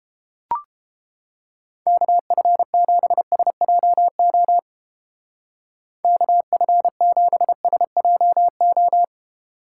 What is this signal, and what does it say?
Morse code, heard as a keyed tone, about 30 wpm, 700 Hz. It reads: KF7SJO KF7SJO